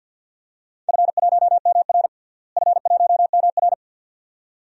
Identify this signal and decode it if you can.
Morse code, heard as a keyed tone, about 47 wpm, 700 Hz. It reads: F1MR F1MR
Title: DXCC-100 Call Signs - 4 Characters - Repeated 2x - 50wpm